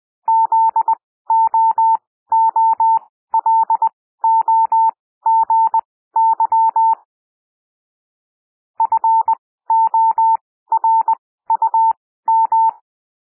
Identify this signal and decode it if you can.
Morse code, heard as a keyed tone, about 20 wpm, spaced as for 14 wpm, 900 Hz. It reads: ZOOLOGY FORUM
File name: SOUND WORDS 20 wpm